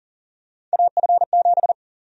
Morse code, heard as a keyed tone, about 40 wpm, 700 Hz. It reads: AF7
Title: USA Call Sign Prefixes - 40wpm